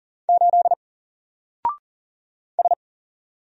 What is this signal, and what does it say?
Morse code, heard as a keyed tone, about 40 wpm, 700 Hz. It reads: 8 S
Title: Single Letter-Number Alt 40wpm